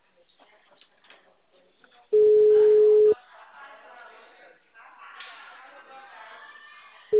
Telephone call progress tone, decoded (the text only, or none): ringback tone